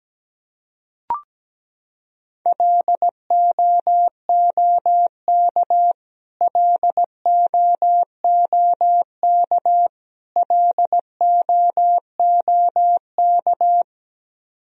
Morse code, heard as a keyed tone, about 17 wpm, 700 Hz. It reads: LOOK LOOK LOOK
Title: Top 100 Words Repeated 17wpm